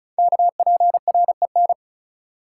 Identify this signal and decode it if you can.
Morse code, heard as a keyed tone, about 36 wpm, 690 Hz. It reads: KPREN
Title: Code Groups - 5 Letters - 35wpm